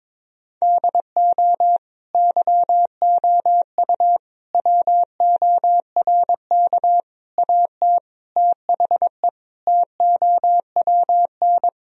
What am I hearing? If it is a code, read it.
Morse code, heard as a keyed tone, about 22 wpm, 700 Hz. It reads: DO YOU WORK AT THE TOWN
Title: Sentences from Top 1000 Words - Review Part 2 - 22wpm